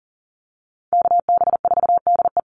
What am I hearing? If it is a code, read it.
Morse code, heard as a keyed tone, about 40 wpm, 700 Hz. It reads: KB4DE
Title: Call Signs - 40wpm